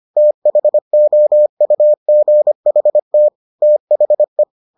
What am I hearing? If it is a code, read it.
Morse code, heard as a keyed tone, about 25 wpm, 600 Hz. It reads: THOUGHT THE